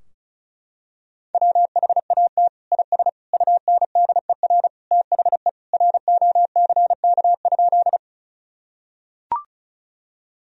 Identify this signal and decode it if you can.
Morse code, heard as a keyed tone, about 36 wpm, 700 Hz. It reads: WHAT IS UNDER THE ROCK?
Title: Sentences from Top 1000 Words - Review Part 2 - 35wpm